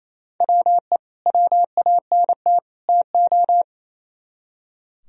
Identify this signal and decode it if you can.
Morse code, heard as a keyed tone, about 34 wpm, 700 Hz. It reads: WE WANT TO
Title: Speed-Racing - Sets of 3 Words 28wpm